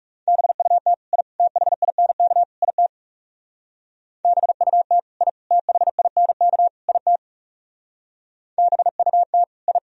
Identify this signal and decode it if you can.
Morse code, heard as a keyed tone, about 37 wpm, 690 Hz. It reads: BUT I THINK IT BUT I THINK IT BUT I
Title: Speed-Racing - Sets of 4 Words 30wpm